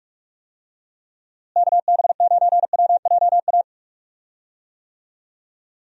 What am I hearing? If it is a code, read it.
Morse code, heard as a keyed tone, about 46 wpm, 700 Hz. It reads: KB9WJA
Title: USA Call Signs - 6 Characters - Repeated 3x - 45wpm